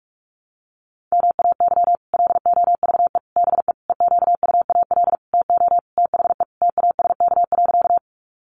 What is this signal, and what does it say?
Morse code, heard as a keyed tone, about 44 wpm, 700 Hz. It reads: MAY LOVE BE EQUAL TO THE TASK.
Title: Sentences from Top 1400 Words 45wpm